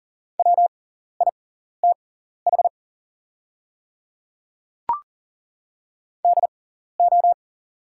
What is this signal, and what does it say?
Morse code, heard as a keyed tone, about 40 wpm, spaced as for 14 wpm, 700 Hz. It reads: WITH DO
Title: 6x Character Spacing - Top 100 Words - 40wpm